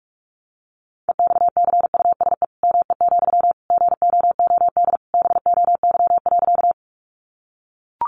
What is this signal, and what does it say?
Morse code, heard as a keyed tone, about 44 wpm, 700 Hz. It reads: EXCUSE ME, GOOD BOY.
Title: Sentences from Top 700 Words 45wpm